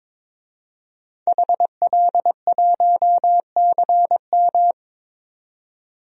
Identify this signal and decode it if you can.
Morse code, heard as a keyed tone, about 22 wpm, 700 Hz. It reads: HL1CM
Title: DXCC-100 Call Signs - 5 Characters - 22wpm